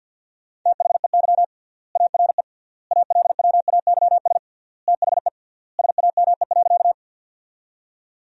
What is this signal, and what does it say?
Morse code, heard as a keyed tone, about 50 wpm, 700 Hz. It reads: THEY ARE ALWAYS THE SAME.
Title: 3x Word-Spacing - Sentences from Top 300 Words - 50wpm